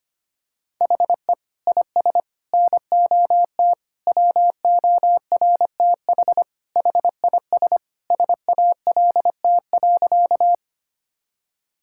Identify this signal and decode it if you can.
Morse code, heard as a keyed tone, about 25 wpm, 700 Hz. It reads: HE IS NOT WORTH HIS SALT.
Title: Sentences from Top 1000 Words - Review Part 5 - 25wpm